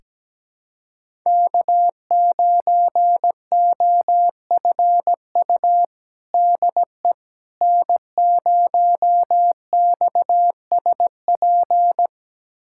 Morse code, heard as a keyed tone, about 17 wpm, 700 Hz. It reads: K9OFU DE N0XSP
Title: USA Callsign DE Callsign - 5 Characters - 17wpm